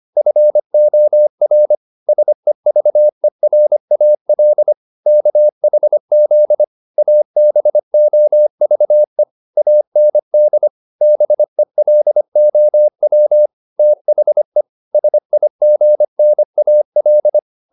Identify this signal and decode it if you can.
Morse code, heard as a keyed tone, about 25 wpm, 600 Hz. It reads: FOR SEVERAL KHZ ABOVE AND BELOW THE SIGNAL